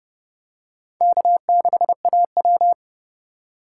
Morse code, heard as a keyed tone, about 30 wpm, 700 Hz. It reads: K6AW
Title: USA Call Signs - 4 Characters - 3x Character Spacing - 30wpm